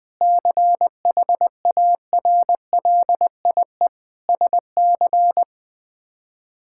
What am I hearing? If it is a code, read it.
Morse code, heard as a keyed tone, about 20 wpm, 700 Hz. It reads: CHARLIE SC
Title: K1USN SST QSO Practice - 20wpm